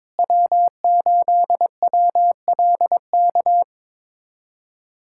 Morse code, heard as a keyed tone, about 22 wpm, 700 Hz. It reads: W8WLK
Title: USA Callsign DE Callsign 22wpm